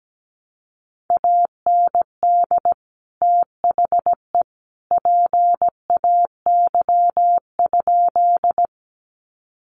Morse code, heard as a keyed tone, about 17 wpm, 700 Hz. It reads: AND THE PAY?
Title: Sentences from Top 700 Words 17wpm